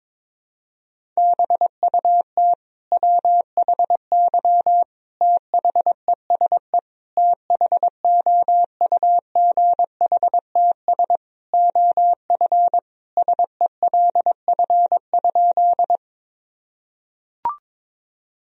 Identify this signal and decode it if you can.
Morse code, heard as a keyed tone, about 21 wpm, 700 Hz. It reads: BUT WHY THESE THOUGHTS OF SELF?
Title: Sentences from Top 1000 Words - Review Part 1 - 22wpm